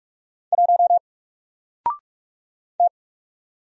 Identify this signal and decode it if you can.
Morse code, heard as a keyed tone, about 46 wpm, 700 Hz. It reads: 1 T
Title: Single Letter-Number 45wpm